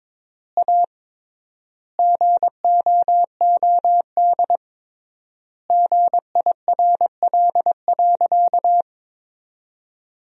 Morse code, heard as a keyed tone, about 21 wpm, 700 Hz. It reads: A GOOD GIRL.
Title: 3x Word-Spacing - Sentences from Top 400 Words - 22wpm